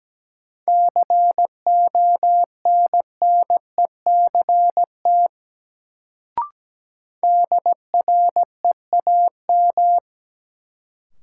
Morse code, heard as a keyed tone, about 17 wpm, 700 Hz. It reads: CONNECT DREAM
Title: Sentences from Top 1500 Words - New Word Drill - 17wpm